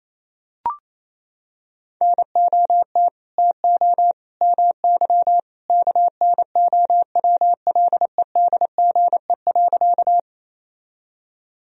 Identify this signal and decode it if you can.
Morse code, heard as a keyed tone, about 28 wpm, 700 Hz. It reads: NOT TO MY KNOWLEDGE.
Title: Sentences from Top 1000 Words - Review Part 3 - 28wpm